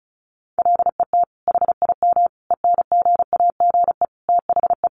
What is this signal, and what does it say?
Morse code, heard as a keyed tone, about 34 wpm, 700 Hz. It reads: LET HIM ENGAGE THE
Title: Sentences from Top 1500 Words 35wpm